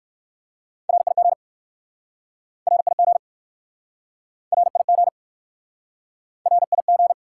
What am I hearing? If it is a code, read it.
Morse code, heard as a keyed tone, about 61 wpm, 700 Hz. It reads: RIG RIG RIG RIG